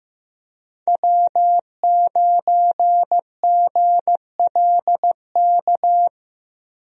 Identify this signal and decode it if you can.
Morse code, heard as a keyed tone, about 15 wpm, 700 Hz. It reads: W9GLK
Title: USA Call Signs - 5 Characters - Repeated 2x - 15wpm